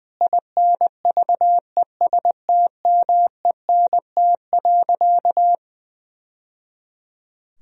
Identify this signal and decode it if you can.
Morse code, heard as a keyed tone, about 20 wpm, 700 Hz. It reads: INVESTMENT.
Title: Sentences from Top 1000 Words - Review Part 4 - 20wpm